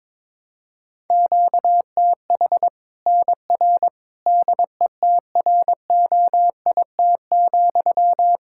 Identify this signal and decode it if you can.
Morse code, heard as a keyed tone, about 22 wpm, 700 Hz. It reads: QTH NR DETROIT,
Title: K1USN SST QSO Practice 22wpm